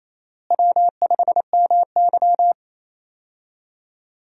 Morse code, heard as a keyed tone, about 28 wpm, 700 Hz. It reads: W5MY
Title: USA Call Signs - 4 Characters - Repeated 3x - 28wpm